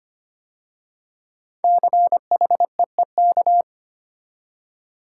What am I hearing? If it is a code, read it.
Morse code, heard as a keyed tone, about 25 wpm, 700 Hz. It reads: CHEEK
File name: 5 Letter Words - First 3 Letters Given - 25wpm